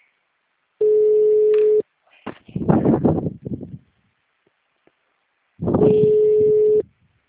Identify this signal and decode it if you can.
Ringback tone (Europe, 425 Hz).
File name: out- -2039- - - .72511